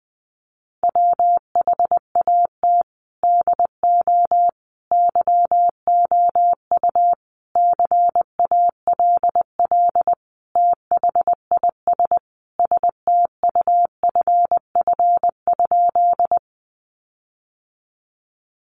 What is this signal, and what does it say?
Morse code, heard as a keyed tone, about 20 wpm, 700 Hz. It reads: WHAT DO YOU CALL THIS STUFF?
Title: Sentences from Top 1400 Words 20wpm